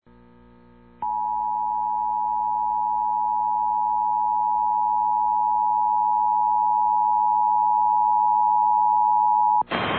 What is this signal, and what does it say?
Emergency Alert System (SAME) tones (the 853 and 960 Hz attention tone). It is heard from 1.0 to 9.7 s.
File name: eas / KSID / KSID